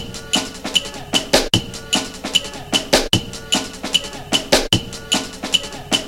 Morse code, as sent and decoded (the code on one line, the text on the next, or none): none